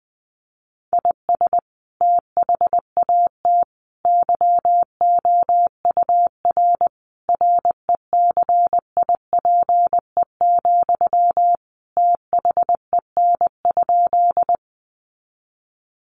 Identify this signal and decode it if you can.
Morse code, heard as a keyed tone, about 20 wpm, 700 Hz. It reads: IS THAT YOUR RECIPE, THEN?
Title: Sentences from Top 1200 Words 20wpm